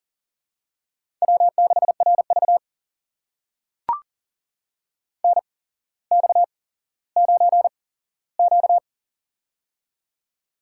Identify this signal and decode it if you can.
Morse code, heard as a keyed tone, about 40 wpm, spaced as for 17 wpm, 700 Hz. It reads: W6RV NX9Q
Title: USA Call Signs - 4 Characters - 8x Character Spacing - 40wpm